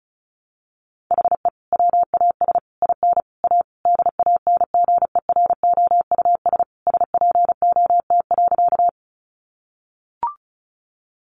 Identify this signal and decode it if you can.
Morse code, heard as a keyed tone, about 34 wpm, 700 Hz. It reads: HE WAS IN A DANGEROUS SPOT.
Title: Sentences from Top 1200 Words 35wpm